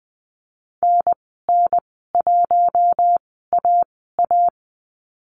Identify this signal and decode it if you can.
Morse code, heard as a keyed tone, about 20 wpm, spaced as for 15 wpm, 700 Hz. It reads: NN1AA
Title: 088 Lesson 15g - Number 5 - Call Signs 20wpm